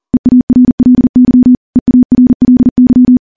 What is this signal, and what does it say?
Morse code, heard as a keyed tone, about 41 wpm, 250 Hz. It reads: EARLY EARLY
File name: morse / morse-training-40WPM